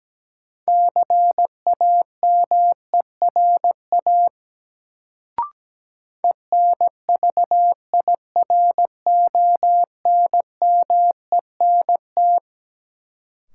Morse code, heard as a keyed tone, about 17 wpm, 700 Hz. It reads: CAMERA ENVIRONMENT
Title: Sentences from Top 1000 Words - Review Part 5 - New Word Drill - 17wpm